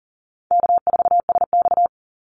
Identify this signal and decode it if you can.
Morse code, heard as a keyed tone, about 38 wpm, 700 Hz. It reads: K4SX
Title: Easy Call Signs - 40wpm